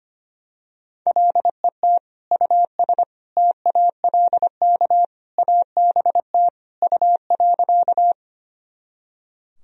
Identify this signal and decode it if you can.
Morse code, heard as a keyed tone, about 25 wpm, 700 Hz. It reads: LET US TALK ABT U.
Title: Sentences from Top 400 Words and Abbreviations 25wpm